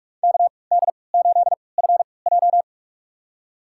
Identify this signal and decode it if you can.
Morse code, heard as a keyed tone, about 46 wpm, spaced as for 26 wpm, 700 Hz. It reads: KD8FJ
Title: USA Call Signs - 5 Characters - 3x Character Spacing - 45wpm